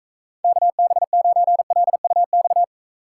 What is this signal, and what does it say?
Morse code, heard as a keyed tone, about 39 wpm, 700 Hz. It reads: KB9LUX